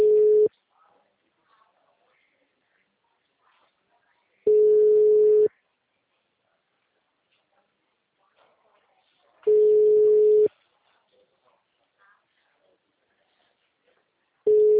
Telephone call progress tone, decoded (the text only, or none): ringback tone